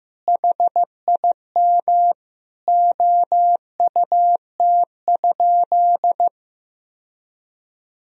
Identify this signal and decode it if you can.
Morse code, heard as a keyed tone, about 15 wpm, 700 Hz. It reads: HIM OUT?
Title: Sentences from Top 1000 Words - Review Part 3 - 15wpm